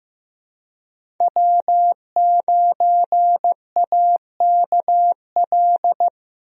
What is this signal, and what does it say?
Morse code, heard as a keyed tone, about 15 wpm, 700 Hz. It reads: W9AKL
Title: USA Call Signs - 5 Characters - 15wpm